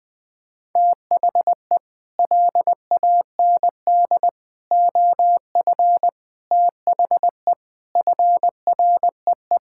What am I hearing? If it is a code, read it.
Morse code, heard as a keyed tone, about 20 wpm, 700 Hz. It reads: THE LAND OF THE FREE
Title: Sets of 9 Words - 20wpm